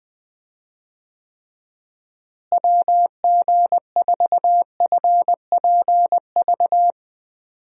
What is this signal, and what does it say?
Morse code, heard as a keyed tone, about 20 wpm, 700 Hz. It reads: WG4FPV
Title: USA Call Signs - 6 Characters - Repeated 2x - 20wpm